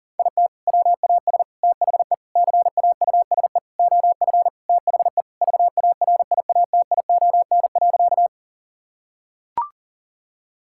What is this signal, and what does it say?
Morse code, heard as a keyed tone, about 37 wpm, 700 Hz. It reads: IT WAS THE CAUSE OF THE VARIATION.
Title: Sentences from Top 1600 Words 40wpm